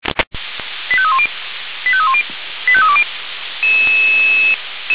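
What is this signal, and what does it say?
Five-tone selective call (ZVEI), heard as quick runs of five tones, tones of about 70 ms. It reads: 85310 85310 85320
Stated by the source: ZVEI five-tone selective call received on SDR, from feuerwehr-orlamuende fire brigade server.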